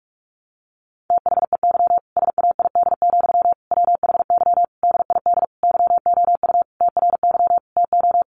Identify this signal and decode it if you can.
Morse code, heard as a keyed tone, about 44 wpm, 700 Hz. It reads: THEY SAID, WHY DID YOU TRY TO
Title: Sentences from Top 1200 Words 45wpm